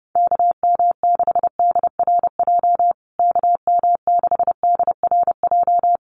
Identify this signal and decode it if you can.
Morse code, heard as a keyed tone, about 30 wpm, 700 Hz. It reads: KM6DRJ KM6DRJ
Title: USA Call Signs Repeated 3x 30wpm